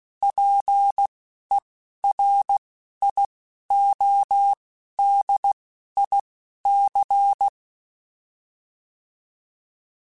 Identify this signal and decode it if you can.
Morse code, heard as a keyed tone, about 16 wpm, spaced as for 12 wpm, 800 Hz. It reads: PERIODIC
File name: fromSMs / FromSMs mp3s / Antennas@10